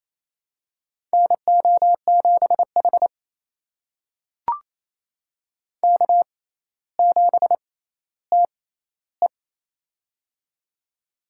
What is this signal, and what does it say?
Morse code, heard as a keyed tone, about 28 wpm, spaced as for 14 wpm, 700 Hz. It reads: NO7H K7TE
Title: USA Call Signs - 6x Character Spacing - 28wpm